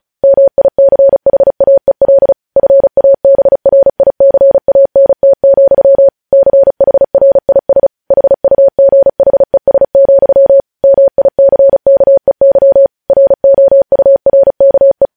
Morse code, heard as a keyed tone, about 36 wpm, 550 Hz. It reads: MICHAEL FABRICANT, CHRIS HUGHES, MICKEY ROURKE
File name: podcast / News headlines-1x-35x35